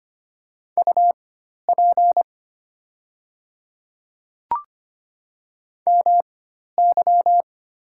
Morse code, heard as a keyed tone, about 25 wpm, spaced as for 12 wpm, 700 Hz. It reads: UP MY
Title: 4x Character Spacing - Top 300 Words - 25wpm